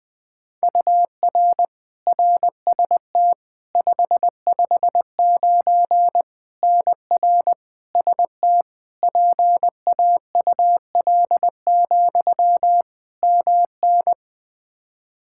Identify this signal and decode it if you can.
Morse code, heard as a keyed tone, about 21 wpm, 700 Hz. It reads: UR RST 559 NR ST PAUL, MN
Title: Practice QSO - Exchange 1 - Beginning - 20wpm